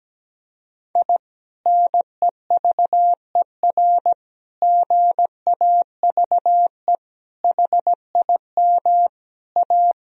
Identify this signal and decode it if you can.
Morse code, heard as a keyed tone, about 17 wpm, 700 Hz. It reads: I NEVER GAVE HIM A
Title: Sentences from Top 1000 Words - Review Part 1 - 17wpm